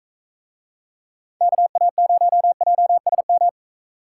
Morse code, heard as a keyed tone, about 41 wpm, 700 Hz. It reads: KA0JSM